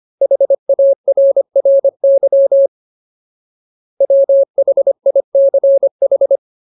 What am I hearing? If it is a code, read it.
Morse code, heard as a keyed tone, about 25 wpm, 550 Hz. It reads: HARRY WHICH
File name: Ultimate Word List / 25WPM Ultimate